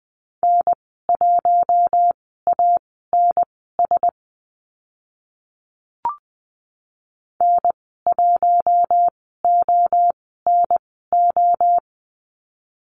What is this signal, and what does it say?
Morse code, heard as a keyed tone, about 20 wpm, spaced as for 14 wpm, 700 Hz. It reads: N1ANS N1ONO
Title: 042 Lesson 8g - Number 1 - Call Signs 20wpm